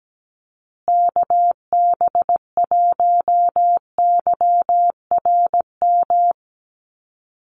Morse code, heard as a keyed tone, about 17 wpm, 700 Hz. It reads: KB1YRM
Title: USA Call Signs Repeated 2x 17wpm